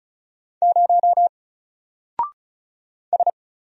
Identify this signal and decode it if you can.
Morse code, heard as a keyed tone, about 36 wpm, 700 Hz. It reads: 0 S
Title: Single Letter-Number Alt 35wpm